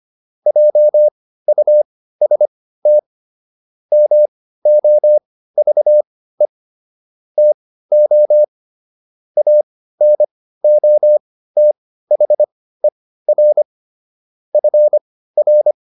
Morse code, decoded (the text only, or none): JUST MOVE TO ANOTHER FR